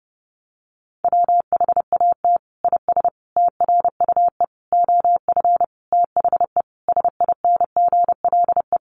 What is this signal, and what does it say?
Morse code, heard as a keyed tone, about 30 wpm, 700 Hz. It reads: WHAT IS TRUE OF THE SINGLE
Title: Sentences from Top 1400 Words 30wpm